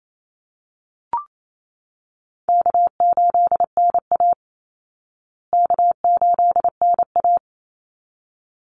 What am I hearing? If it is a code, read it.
Morse code, heard as a keyed tone, about 28 wpm, 700 Hz. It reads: K8NA K8NA
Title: Easy Call Signs Repeated 2x - 28wpm